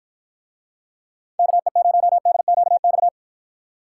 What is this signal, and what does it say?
Morse code, heard as a keyed tone, about 49 wpm, 700 Hz. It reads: KE0DQX